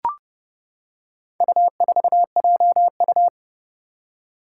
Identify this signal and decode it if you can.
Morse code, heard as a keyed tone, about 31 wpm, 700 Hz. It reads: U4JU
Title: Four Challenging Letter-Number Combinations 30wpm